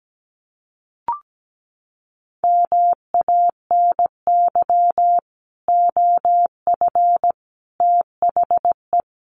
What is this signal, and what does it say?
Morse code, heard as a keyed tone, about 17 wpm, 700 Hz. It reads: MANY OF THE
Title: Sets of 3 Words 17wpm